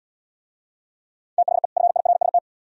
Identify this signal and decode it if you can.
Morse code, heard as a keyed tone, about 71 wpm, 700 Hz. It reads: THE FIRST